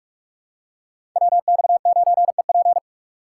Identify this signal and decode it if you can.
Morse code, heard as a keyed tone, about 43 wpm, 700 Hz. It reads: WX9EP
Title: USA Call Signs - 5 Characters - 4x Character Spacing - 45wpm